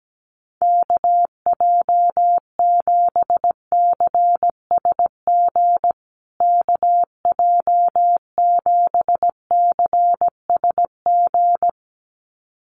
Morse code, heard as a keyed tone, about 17 wpm, 700 Hz. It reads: KJ7CSG KJ7CSG
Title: USA Call Signs Repeated 2x 17wpm